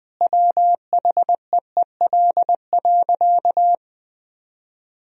Morse code, heard as a keyed tone, about 20 wpm, 700 Hz. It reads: WHEEL.
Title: Sentences from Top 1000 Words - Review Part 3 - 20wpm